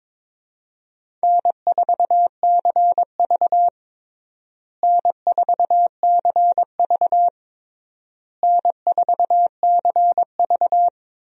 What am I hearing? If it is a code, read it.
Morse code, heard as a keyed tone, about 21 wpm, 700 Hz. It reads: N4CV N4CV N4CV
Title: USA Call Signs - 4 Characters - Repeated 3x - 22wpm